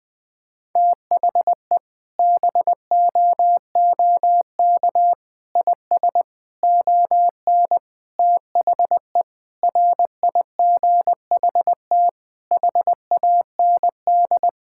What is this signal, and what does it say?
Morse code, heard as a keyed tone, about 20 wpm, 700 Hz. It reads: THE BOOK IS ON THE RIGHT HAND
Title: Sentences with 7-Word Phrases - Spoken Beginning - 20wpm